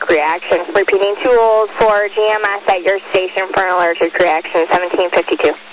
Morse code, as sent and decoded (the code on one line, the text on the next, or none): none